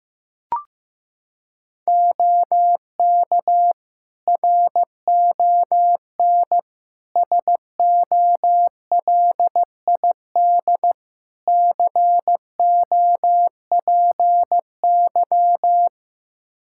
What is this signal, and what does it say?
Morse code, heard as a keyed tone, about 15 wpm, 700 Hz. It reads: OK RON SOLID COPY
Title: Practice QSO - Exchange 3 - More Info - 15wpm